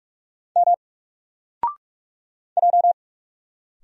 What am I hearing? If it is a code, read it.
Morse code, heard as a keyed tone, about 44 wpm, 700 Hz. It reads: M J
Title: Single Letter-Number Alt 45wpm